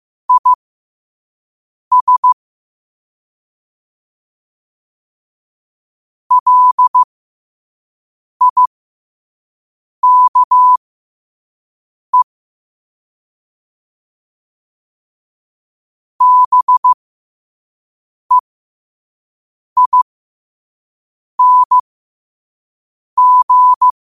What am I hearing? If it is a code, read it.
Morse code, heard as a keyed tone, about 15 wpm, spaced as for 5 wpm, 1000 Hz. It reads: IS LIKE BEING
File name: qotdfiles / 5wpm